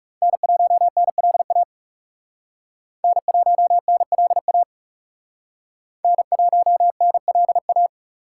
Morse code, heard as a keyed tone, about 40 wpm, 700 Hz. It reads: N1NLA N1NLA N1NLA